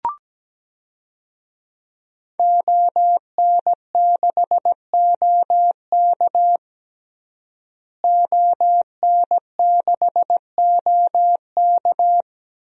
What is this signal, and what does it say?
Morse code, heard as a keyed tone, about 17 wpm, 700 Hz. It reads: ON6OK ON6OK
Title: DXCC-100 Call Signs - 5 Characters - Repeated 2x - 17wpm